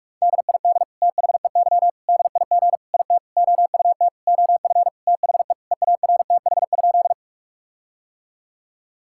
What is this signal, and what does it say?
Morse code, heard as a keyed tone, about 46 wpm, 700 Hz. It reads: DID THEY DIG IT OUT OF THE EARTH?
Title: Sentences from Top 1600 Words 45wpm